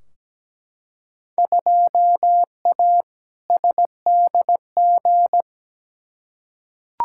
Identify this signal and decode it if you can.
Morse code, heard as a keyed tone, about 17 wpm, 700 Hz. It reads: 2A SDG
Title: Field Day Exchanges Encore - 17wpm